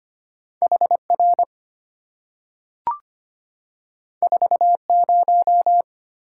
Morse code, heard as a keyed tone, about 25 wpm, 700 Hz. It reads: HR 40
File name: Two Letter-Number Combinations - 25wpm